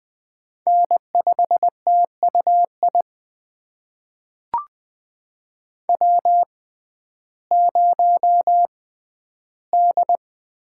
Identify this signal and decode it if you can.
Morse code, heard as a keyed tone, about 20 wpm, spaced as for 12 wpm, 700 Hz. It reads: N5TUI W0D
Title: USA Call Signs - 5 Characters - 6x Character Spacing - 20wpm